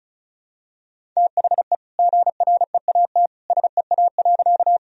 Morse code, heard as a keyed tone, about 36 wpm, 700 Hz. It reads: THE GREAT SEA.
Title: Sentences from Top 1000 Words - Review Part 1 - 35wpm